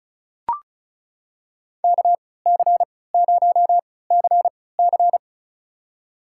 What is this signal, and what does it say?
Morse code, heard as a keyed tone, about 34 wpm, spaced as for 20 wpm, 700 Hz. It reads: KC0CC
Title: USA Call Signs - 3x Character Spacing - 35wpm